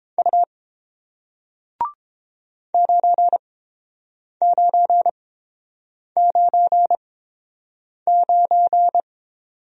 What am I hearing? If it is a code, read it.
Morse code, heard as a keyed tone, about 30 wpm, spaced as for 7 wpm, 700 Hz. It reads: U 9999